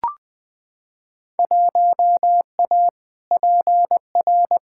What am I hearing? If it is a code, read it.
Morse code, heard as a keyed tone, about 20 wpm, 700 Hz. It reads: 1A PR
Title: Field Day Exchanges Encore - 20wpm